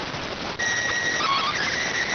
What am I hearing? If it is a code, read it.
SSTV (slow-scan television), PD 290.